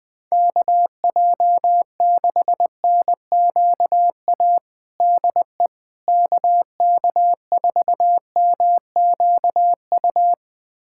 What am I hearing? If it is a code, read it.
Morse code, heard as a keyed tone, about 20 wpm, 700 Hz. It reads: KJ6NQA DE KK4MQU
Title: USA Callsign DE Callsign 20wpm